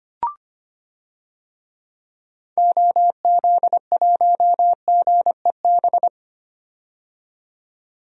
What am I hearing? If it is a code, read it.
Morse code, heard as a keyed tone, about 25 wpm, 700 Hz. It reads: OZ1GEB
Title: DXCC-100 Call Signs - 6 Characters - 25wpm